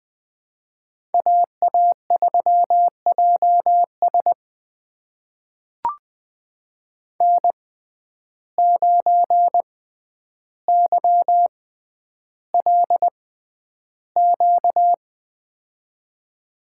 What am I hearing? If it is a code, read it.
Morse code, heard as a keyed tone, about 20 wpm, spaced as for 10 wpm, 700 Hz. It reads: AA3JS N9YLQ
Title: USA Call Signs - 5 Characters - 6x Character Spacing - 20wpm